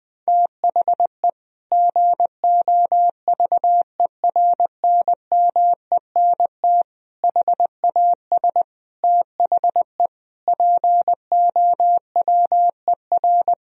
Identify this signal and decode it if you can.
Morse code, heard as a keyed tone, about 20 wpm, 700 Hz. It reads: THE GOVERNMENT HAS THE POWER
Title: Sentences from Top 1000 Words - Review Part 3 - 20wpm